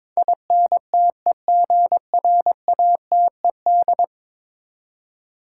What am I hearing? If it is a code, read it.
Morse code, heard as a keyed tone, about 22 wpm, 700 Hz. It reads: INTEGRATED
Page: morse-code-ninja-practice-22wpm